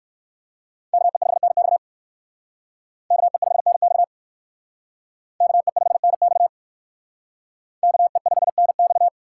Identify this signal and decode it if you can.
Morse code, heard as a keyed tone, about 61 wpm, 700 Hz. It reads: KE5NX KE5NX KE5NX KE5NX